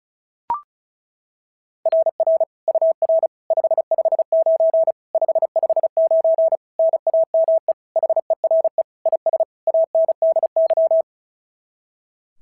Morse code, heard as a keyed tone, about 36 wpm, 660 Hz. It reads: RR UR 559 559 NAME HERE IS ANDY
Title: SOTA QSO Practice 35wpm